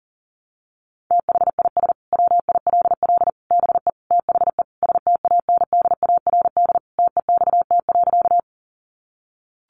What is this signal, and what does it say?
Morse code, heard as a keyed tone, about 40 wpm, 700 Hz. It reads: THIS WILL BE THE STANDARD TEXT.
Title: Sentences from Top 1200 Words 40wpm